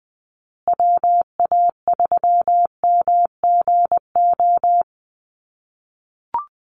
Morse code, heard as a keyed tone, about 20 wpm, 700 Hz. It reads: WA3MGO
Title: USA Call Signs Repeated 3x 20wpm